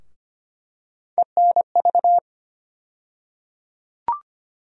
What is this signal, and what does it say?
Morse code, heard as a keyed tone, about 25 wpm, 700 Hz. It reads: ENV
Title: USA Call Sign Postfixes - 3 Letters 25wpm